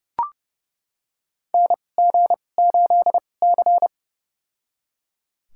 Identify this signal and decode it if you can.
Morse code, heard as a keyed tone, about 31 wpm, spaced as for 22 wpm, 700 Hz. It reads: NG8C
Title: USA Call Signs - 4 Characters - 2x Character Spacing - 30wpm